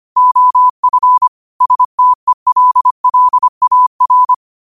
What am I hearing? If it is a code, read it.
Morse code, heard as a keyed tone, about 25 wpm, 1000 Hz. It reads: OF STELLAR
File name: qotdfiles / 25wpm